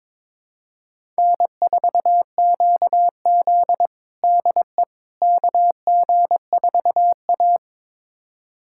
Morse code, heard as a keyed tone, about 21 wpm, 700 Hz. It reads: N4QZ DE KG4A
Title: USA Callsign DE Callsign - 4 Characters - 22wpm